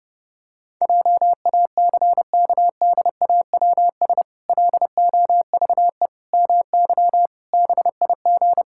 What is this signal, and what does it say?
Morse code, heard as a keyed tone, about 28 wpm, 700 Hz. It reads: JACKDAWS LOVE MY BIG
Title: Pangram Challenge v2 - 30wpm